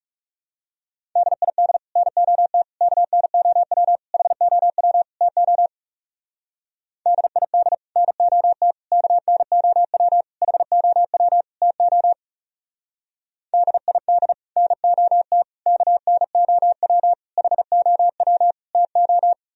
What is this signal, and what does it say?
Morse code, heard as a keyed tone, about 34 wpm, 700 Hz. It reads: DID NOT KNOW HOW TO DID NOT KNOW HOW TO DID NOT KNOW HOW TO
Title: Speed-Racing - Sets of 5 Words 30wpm